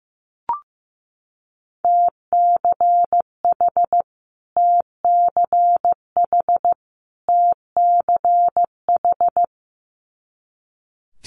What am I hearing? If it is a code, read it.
Morse code, heard as a keyed tone, about 15 wpm, 700 Hz. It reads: TCH TCH TCH
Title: Multi-letter Phonemes Repeated 3x 15wpm